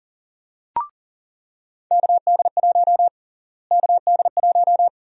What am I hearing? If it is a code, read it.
Morse code, heard as a keyed tone, about 38 wpm, 700 Hz. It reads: KD1 KD1
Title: USA Call Sign Prefixes - Repeated 2x - 40wpm